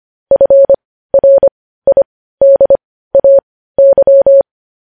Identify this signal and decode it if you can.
Morse code, heard as a keyed tone, about 25 wpm, spaced as for 15 wpm, 550 Hz. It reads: FRIDAY
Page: news headlines in morse code at 15 wpm